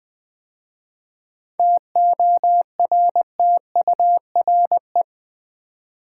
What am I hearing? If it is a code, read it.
Morse code, heard as a keyed tone, about 20 wpm, 700 Hz. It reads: TORTURE
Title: 7 Letter Words - First 3 Letters Given - 20wpm